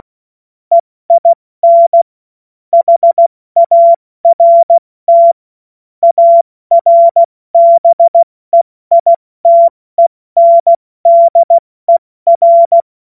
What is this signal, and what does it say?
Morse code, heard as a keyed tone, about 16 wpm, 680 Hz. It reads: EIN HART ARBEITENDER
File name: Morsekurs / Witze / Witz 18. April 70 F80